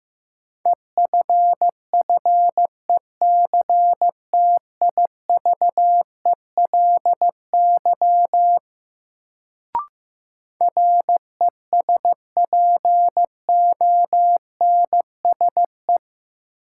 Morse code, heard as a keyed tone, about 15 wpm, 700 Hz. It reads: EFFECTIVELY RESPONSE
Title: Sentences from Top 1400 Words - New Word Drill Alt - 15wpm